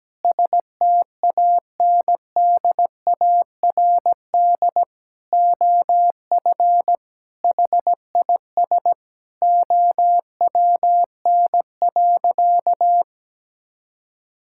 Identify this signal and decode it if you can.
Morse code, heard as a keyed tone, about 17 wpm, 700 Hz. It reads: STANDARD OF HIS OWN.
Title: Sentences from Top 1000 Words - Review Part 6 - 17wpm